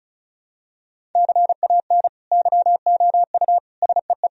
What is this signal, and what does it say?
Morse code, heard as a keyed tone, about 34 wpm, 700 Hz. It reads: CAN YOU SEE
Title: Sentences from Top 1000 Words - Review Part 1 - 35wpm